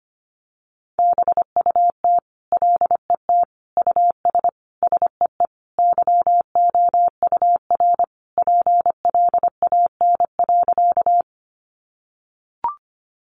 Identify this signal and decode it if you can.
Morse code, heard as a keyed tone, about 25 wpm, 700 Hz. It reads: BUT LET US SEE YOUR PLAN.
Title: Sentences from Top 500 Words 25wpm